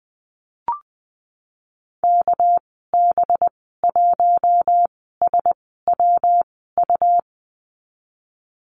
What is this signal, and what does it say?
Morse code, heard as a keyed tone, about 20 wpm, spaced as for 15 wpm, 700 Hz. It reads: KB1SWU
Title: 207 Lesson 33g - Letter j - Call Signs 20wpm